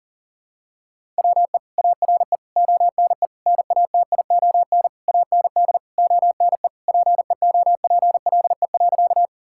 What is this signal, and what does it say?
Morse code, heard as a keyed tone, about 40 wpm, 700 Hz. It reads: WE ARE ONE NATION AND ONE PEOPLE.
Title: Sentences from Top 1000 Words - Review Part 6 - 40wpm